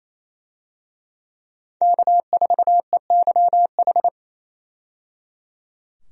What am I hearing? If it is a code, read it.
Morse code, heard as a keyed tone, about 28 wpm, 700 Hz. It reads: K4EYH
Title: USA Call Signs - 5 Characters - 28wpm